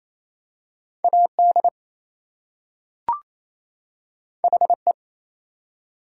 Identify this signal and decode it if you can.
Morse code, heard as a keyed tone, about 27 wpm, 700 Hz. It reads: AD HE
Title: Two Letter Words - 28wpm